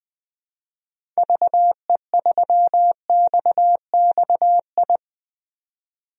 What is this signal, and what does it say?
Morse code, heard as a keyed tone, about 20 wpm, 700 Hz. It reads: VE3XXI
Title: DXCC-100 Call Signs - 6 Characters - 20wpm